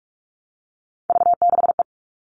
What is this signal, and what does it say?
Morse code, heard as a keyed tone, about 44 wpm, 700 Hz. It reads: V6E
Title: All Cut Numbers - 3-Digit 45wpm